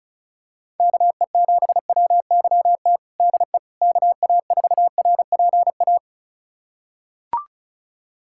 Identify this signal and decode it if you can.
Morse code, heard as a keyed tone, about 34 wpm, 700 Hz. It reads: KE7WYT DE KA4RPA
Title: USA Callsign DE Callsign 35wpm